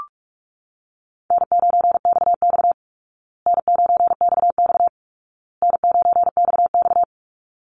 Morse code, heard as a keyed tone, about 44 wpm, 700 Hz. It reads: N9XX N9XX N9XX
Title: Call Signs Repeated 2x - 45wpm